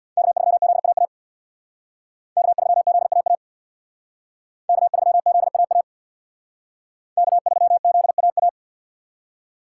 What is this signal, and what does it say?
Morse code, heard as a keyed tone, about 62 wpm, 700 Hz. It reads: K3ZAA K3ZAA K3ZAA K3ZAA
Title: Speed Racing - USA 1x3 Call Signs 50wpm